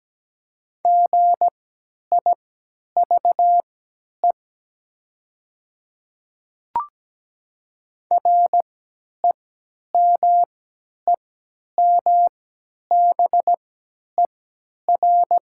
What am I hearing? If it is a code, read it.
Morse code, heard as a keyed tone, about 17 wpm, spaced as for 10 wpm, 700 Hz. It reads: GIVE REMEMBER
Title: 3x Character Spacing - Top 500 Words - 17wpm